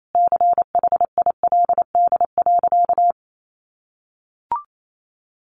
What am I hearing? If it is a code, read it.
Morse code, heard as a keyed tone, about 28 wpm, 700 Hz. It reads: CHILD.
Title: Sentences from Top 1100 Words 28wpm